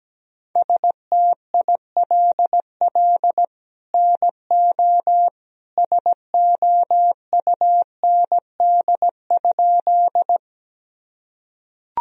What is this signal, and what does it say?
Morse code, heard as a keyed tone, about 17 wpm, 700 Hz. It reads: STILL NO SOUND?
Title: Sentences from Top 1000 Words - Review Part 1 - 17wpm